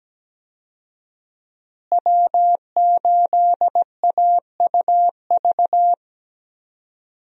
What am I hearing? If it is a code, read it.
Morse code, heard as a keyed tone, about 17 wpm, 700 Hz. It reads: W8AUV
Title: USA Call Signs - 5 Characters - Repeated 2x - 17wpm